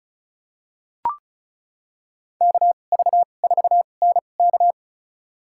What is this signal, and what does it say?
Morse code, heard as a keyed tone, about 36 wpm, spaced as for 25 wpm, 700 Hz. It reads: KV4NK
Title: USA Call Signs - 2x Character Spacing - 35wpm